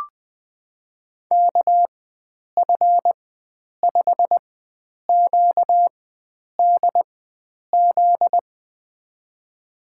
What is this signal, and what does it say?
Morse code, heard as a keyed tone, about 20 wpm, spaced as for 9 wpm, 700 Hz. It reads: KF5QDZ
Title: USA Call Signs - 6 Characters - 4x Character Spacing - 20wpm